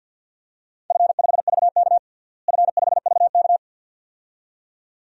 Morse code, heard as a keyed tone, about 53 wpm, 700 Hz. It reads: F5VK F5VK
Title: DXCC-100 Call Signs - 4 Characters - Repeated 2x - 50wpm